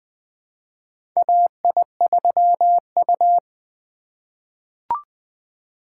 Morse code, heard as a keyed tone, about 20 wpm, 700 Hz. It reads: AI3U
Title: USA Call Signs - 2x Character Spacing - 20wpm